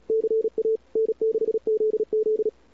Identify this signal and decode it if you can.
Morse code, heard as a keyed tone, about 35 wpm, 410 Hz. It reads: CA N6ZZ
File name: cw-sprint-k5tr-op-n5rz